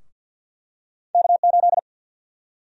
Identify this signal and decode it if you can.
Morse code, heard as a keyed tone, about 52 wpm, 700 Hz. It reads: K8
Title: USA Call Sign Prefixes - 50wpm